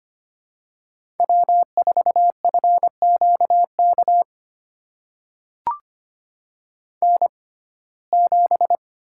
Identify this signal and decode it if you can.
Morse code, heard as a keyed tone, about 25 wpm, spaced as for 18 wpm, 700 Hz. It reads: W4FQK N7
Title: USA Call Signs - 5 Characters - 6x Character Spacing - 25wpm